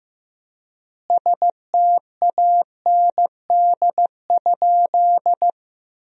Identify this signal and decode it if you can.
Morse code, heard as a keyed tone, about 15 wpm, 700 Hz. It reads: STAND?
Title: 4x Word-Spacing - Sentences from Top 300 Words - 15wpm